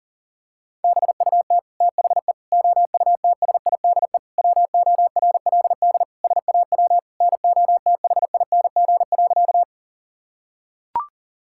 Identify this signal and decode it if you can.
Morse code, heard as a keyed tone, about 40 wpm, 700 Hz. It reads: BUT THE OUTSIDE WORLD SAW NOTHING.
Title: Sentences from Top 1000 Words - Review Part 6 - 40wpm